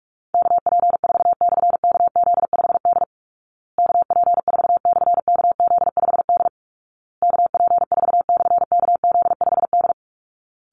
Morse code, heard as a keyed tone, about 44 wpm, 700 Hz. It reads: KP4/KZ5D KP4/KZ5D KP4/KZ5D
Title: Call Signs Repeated 2x - 45wpm